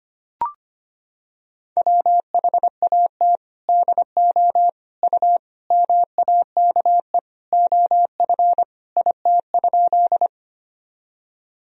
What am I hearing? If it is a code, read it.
Morse code, heard as a keyed tone, about 25 wpm, 700 Hz. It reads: WHAT DO U MAKE OF IT?
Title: Sentences from Top 100 Words and Abbreviations 25wpm